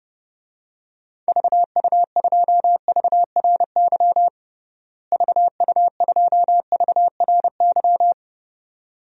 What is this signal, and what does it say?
Morse code, heard as a keyed tone, about 32 wpm, 700 Hz. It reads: VU2VRY VU2VRY
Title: DXCC-100 Call Signs - 6 Characters - Repeated 2x - 30wpm